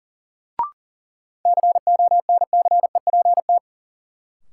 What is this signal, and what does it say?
Morse code, heard as a keyed tone, about 40 wpm, 700 Hz. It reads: CONCEPT
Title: Sentences from Top 1100 Words - New Word Drill Alt - 40wpm